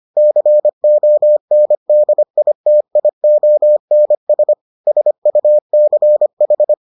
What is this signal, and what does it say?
Morse code, heard as a keyed tone, about 25 wpm, 600 Hz. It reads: CONDITIONS SUCH